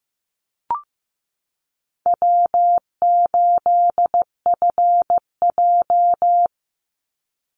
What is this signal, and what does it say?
Morse code, heard as a keyed tone, about 15 wpm, 700 Hz. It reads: W8FJ
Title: Easy Call Signs - 15wpm